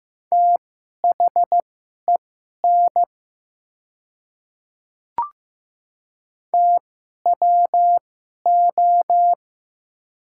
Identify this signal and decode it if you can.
Morse code, heard as a keyed tone, about 15 wpm, spaced as for 11 wpm, 700 Hz. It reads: THEN TWO
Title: 2x Character Spacing - Top 100 Words - 15wpm